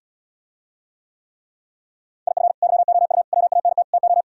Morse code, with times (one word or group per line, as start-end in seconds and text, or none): 2.3-2.5 IF
2.6-3.2 YOU
3.3-3.8 WANT
3.9-4.2 TO